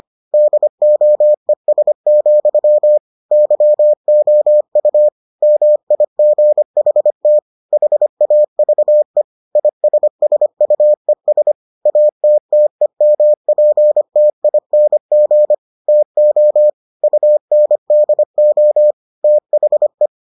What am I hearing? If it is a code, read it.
Morse code, heard as a keyed tone, about 25 wpm, 600 Hz. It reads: DOES, YOU MIGHT HAVE ISSUES ATTEMPTING TO UNDO THE